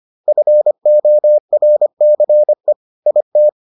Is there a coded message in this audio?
Yes: Morse code from 0.3 to 3.5 s, heard as a keyed tone, about 25 wpm, 600 Hz. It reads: FORCE IT